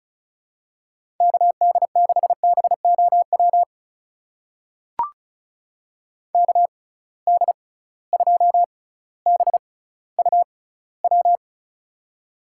Morse code, heard as a keyed tone, about 36 wpm, spaced as for 18 wpm, 700 Hz. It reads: KD6BOW KD2BUW
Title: USA Call Signs - 6 Characters - 6x Character Spacing - 35wpm